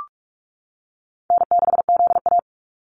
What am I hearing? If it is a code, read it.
Morse code, heard as a keyed tone, about 46 wpm, 700 Hz. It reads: N6ZA
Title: NA SOTA Activator Callsigns 45wpm